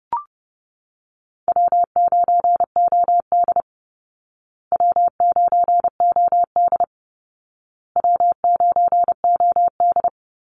Morse code, heard as a keyed tone, about 28 wpm, 700 Hz. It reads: W9OD W9OD W9OD
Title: Easy Call Signs Repeated 3x - 30wpm